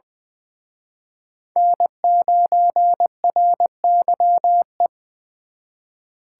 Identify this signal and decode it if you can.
Morse code, heard as a keyed tone, about 25 wpm, 700 Hz. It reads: N9RYE
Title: Speed Racing - USA 1x3 Call Signs 20wpm